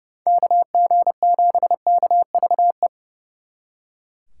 Morse code, heard as a keyed tone, about 36 wpm, 700 Hz. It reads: KG7KVE
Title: Speed Racing - USA 2x3 Call Signs 30wpm